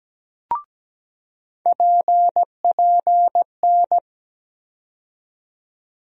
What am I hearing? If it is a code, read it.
Morse code, heard as a keyed tone, about 17 wpm, 700 Hz. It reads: PPN
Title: USA Call Sign Postfixes - 3 Letters 17wpm